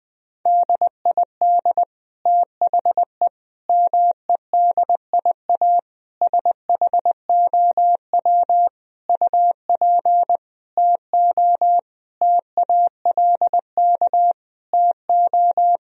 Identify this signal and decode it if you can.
Morse code, heard as a keyed tone, about 20 wpm, 700 Hz. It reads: DID THE MEDIA SHOW UP TO TALK TO
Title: Sentences from Top 1000 Words - Review Part 4 - 20wpm